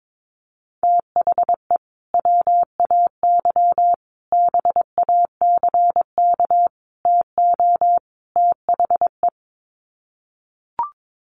Morse code, heard as a keyed tone, about 22 wpm, 700 Hz. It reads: THE WAY BACK TO THE
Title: Sets of 5 Words Encore2 22wpm